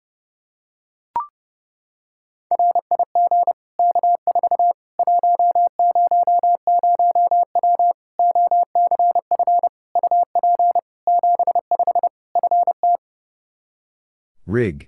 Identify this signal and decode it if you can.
Morse code, heard as a keyed tone, about 32 wpm, 700 Hz. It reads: RIG K4 100W OCF UP 75 FT
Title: Practice QSO - Exchange 2 - Info - 30wpm